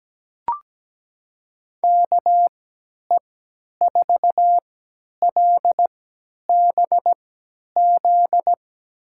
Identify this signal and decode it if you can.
Morse code, heard as a keyed tone, about 17 wpm, spaced as for 10 wpm, 700 Hz. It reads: KE4LBZ
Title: USA Call Signs - 6 Characters - 3x Character Spacing - 17wpm